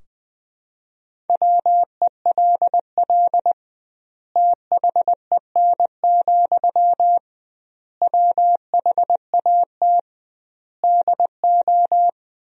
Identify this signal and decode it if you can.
Morse code, heard as a keyed tone, about 21 wpm, 700 Hz. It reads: WELL THEN, WHAT DO
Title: 2x Word-Spacing - Sentences from Top 200 Words - 20wpm